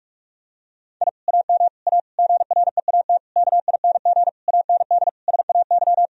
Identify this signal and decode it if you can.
Morse code, heard as a keyed tone, about 46 wpm, 700 Hz. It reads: I AM A GREAT KING AND SAY
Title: Sentences from Top 1000 Words - Review Part 4 - 45wpm